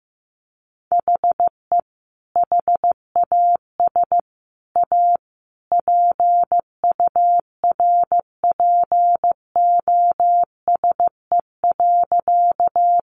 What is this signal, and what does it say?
Morse code, heard as a keyed tone, about 15 wpm, 700 Hz. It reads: HE HAS A PURPOSE.
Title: Sentences from Top 800 Words 15wpm